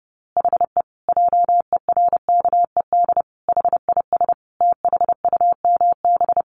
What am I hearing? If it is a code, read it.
Morse code, heard as a keyed tone, about 30 wpm, 700 Hz. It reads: HE JERKED HIS THUMB
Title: A Sound Of Thunder 30wpm